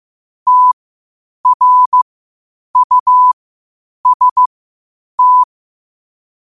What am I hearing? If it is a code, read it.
Morse code, heard as a keyed tone, about 15 wpm, spaced as for 8 wpm, 1000 Hz. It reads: TRUST
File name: qotdfiles / 8wpm